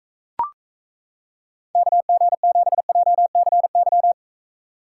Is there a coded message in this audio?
Yes: Morse code from 1.8 to 4.1 s, heard as a keyed tone, about 36 wpm, 700 Hz. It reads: KG7JCY